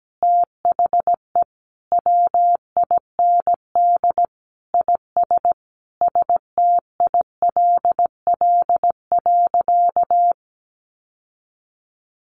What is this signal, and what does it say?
Morse code, heard as a keyed tone, about 17 wpm, 700 Hz. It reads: THE WIND IS STILL.
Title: Sentences from Top 1200 Words 17wpm